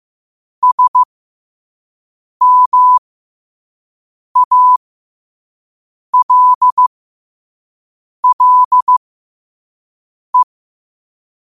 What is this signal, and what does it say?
Morse code, heard as a keyed tone, about 15 wpm, spaced as for 5 wpm, 1000 Hz. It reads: SMALLE
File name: qotdfiles / 5wpm